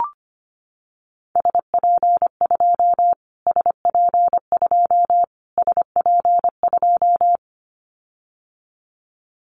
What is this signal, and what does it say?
Morse code, heard as a keyed tone, about 25 wpm, 700 Hz. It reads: SP2 SP2 SP2
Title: Callsign Prefixes for Top 25 DXCC - Repeated 3x - 25wpm